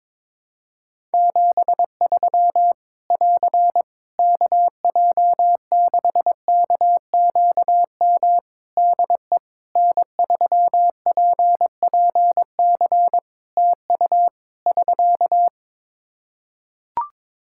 Morse code, heard as a keyed tone, about 22 wpm, 700 Hz. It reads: 73 <AR> KJ6KQM DE N3PPC TU <SK>
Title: K1USN SST QSO Practice 22wpm